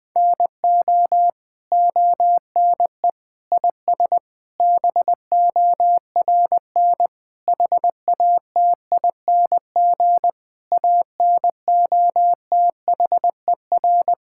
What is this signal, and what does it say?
Morse code, heard as a keyed tone, about 21 wpm, 700 Hz. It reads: NO ONE IS BORN HATING ANOTHER
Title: Sentences with 9-Word Phrases - Spoken Beginning - 20wpm